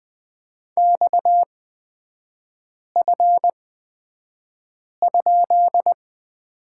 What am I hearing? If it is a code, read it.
Morse code, heard as a keyed tone, about 20 wpm, spaced as for 5 wpm, 700 Hz. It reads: XF?
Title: 243 Lesson 39e - Letter z - Three Characters 20wpm